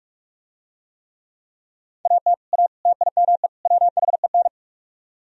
Morse code, heard as a keyed tone, about 44 wpm, 700 Hz. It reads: AT A TIME WHEN